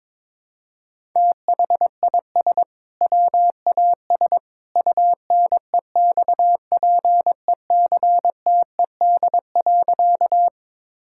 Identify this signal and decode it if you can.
Morse code, heard as a keyed tone, about 21 wpm, 700 Hz. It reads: THIS WAS UNEXPECTED.
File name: practice / 22wpm / Sentences with 2-Word Phrases - 22wpm